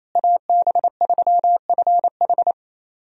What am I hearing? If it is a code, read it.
Morse code, heard as a keyed tone, about 28 wpm, 700 Hz. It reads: AB3FH
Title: USA Call Signs - 5 Characters - Repeated 2x - 28wpm